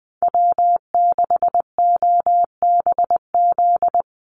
Morse code, heard as a keyed tone, about 20 wpm, 700 Hz. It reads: W6OBZ
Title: USA Call Signs 20wpm